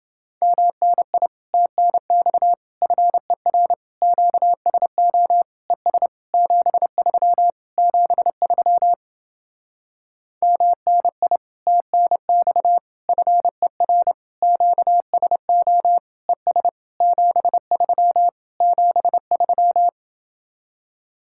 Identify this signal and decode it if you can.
Morse code, heard as a keyed tone, about 30 wpm, 700 Hz. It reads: MNI TNX FER QSO ES 73 73 MNI TNX FER QSO ES 73 73